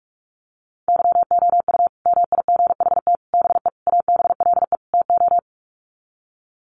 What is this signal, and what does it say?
Morse code, heard as a keyed tone, about 44 wpm, 700 Hz. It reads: YOU MIGHT BE ABLE TO
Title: Sets of 5 Words Encore2 45wpm